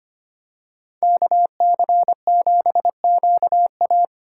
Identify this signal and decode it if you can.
Morse code, heard as a keyed tone, about 25 wpm, 700 Hz. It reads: KC7QA
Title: USA Call Signs - 5 Characters - 2x Character Spacing - 25wpm